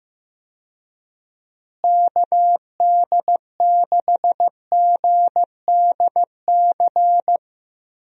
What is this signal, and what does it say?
Morse code, heard as a keyed tone, about 15 wpm, 700 Hz. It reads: KD6GDC
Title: USA Call Signs - 6 Characters - Repeated 2x - 15wpm